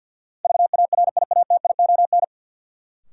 Morse code, heard as a keyed tone, about 52 wpm, 700 Hz. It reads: VARIATION
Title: Sentences from Top 1600 Words - New Word Drill Alt - 50wpm